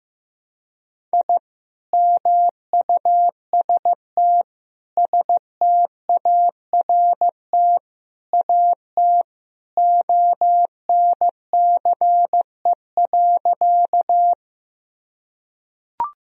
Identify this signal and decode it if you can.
Morse code, heard as a keyed tone, about 16 wpm, 700 Hz. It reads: I MUST START AT ONCE.
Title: Sentences from Top 1000 Words - Review Part 1 - 15wpm